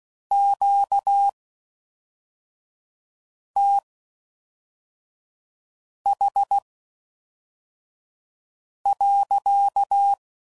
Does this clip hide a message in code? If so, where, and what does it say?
Yes: Morse code from 0.3 to 10.1 s, heard as a keyed tone, about 16 wpm, spaced as for 4 wpm, 800 Hz. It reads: QTH.